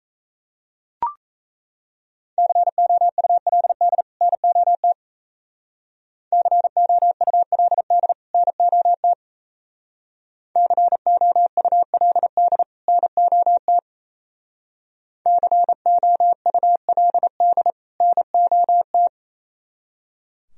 Morse code, heard as a keyed tone, about 36 wpm, 700 Hz. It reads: COULD NOT COULD NOT COULD NOT COULD NOT